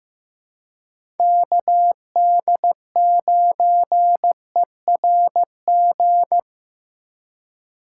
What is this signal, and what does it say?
Morse code, heard as a keyed tone, about 15 wpm, 700 Hz. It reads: KD9ERG
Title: USA Call Signs - 6 Characters - 2x Character Spacing - 15wpm